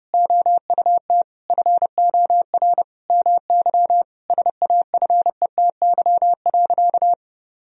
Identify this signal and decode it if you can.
Morse code, heard as a keyed tone, about 30 wpm, 700 Hz. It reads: OUT FOR MY SAFETY.
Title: Sentences from Top 1000 Words - Review Part 4 - 30wpm